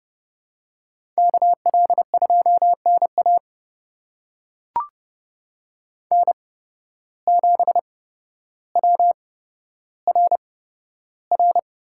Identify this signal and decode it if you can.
Morse code, heard as a keyed tone, about 30 wpm, spaced as for 13 wpm, 700 Hz. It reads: KL2NA N7WRR
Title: USA Call Signs - 5 Characters - 8x Character Spacing - 30wpm